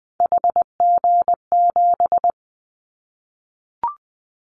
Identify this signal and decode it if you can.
Morse code, heard as a keyed tone, about 20 wpm, 700 Hz. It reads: HG7
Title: Callsign Prefixes for Top 25 DXCC - 20wpm